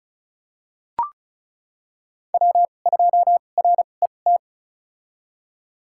Morse code, heard as a keyed tone, about 34 wpm, spaced as for 25 wpm, 700 Hz. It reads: W2RET
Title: USA Call Signs - 5 Characters - 2x Character Spacing - 35wpm